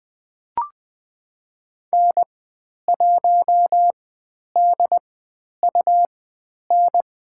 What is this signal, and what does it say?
Morse code, heard as a keyed tone, about 20 wpm, spaced as for 10 wpm, 700 Hz. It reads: N1DUN
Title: 102 Lesson 17g - Letter u - Call Signs 20wpm